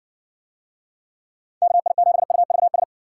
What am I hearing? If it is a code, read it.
Morse code, heard as a keyed tone, about 54 wpm, 700 Hz. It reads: KI7UVS